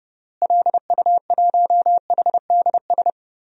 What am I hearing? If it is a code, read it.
Morse code, heard as a keyed tone, about 32 wpm, 700 Hz. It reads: LU1HDS
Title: DXCC-100 Call Signs - 6 Characters - Repeated 3x - 30wpm